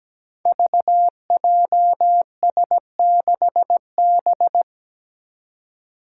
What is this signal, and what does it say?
Morse code, heard as a keyed tone, about 17 wpm, 700 Hz. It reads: VJS6B
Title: Five Challenging Letter-Number Combinations 17wpm